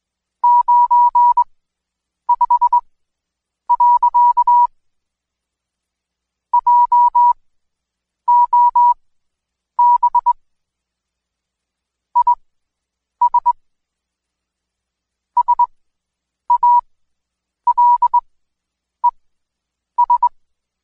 Morse code, heard as a keyed tone, about 22 wpm, spaced as for 8 wpm, 970 Hz. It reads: 95. JOB IS SALES
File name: Code Quick / AudioDownld2 / Seg114-Day19-DayDrill 19